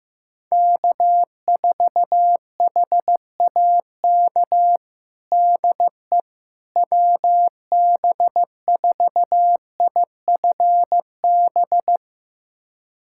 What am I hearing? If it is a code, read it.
Morse code, heard as a keyed tone, about 15 wpm, 700 Hz. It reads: K4HAK DE WB4IFB
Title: USA Callsign DE Callsign 15wpm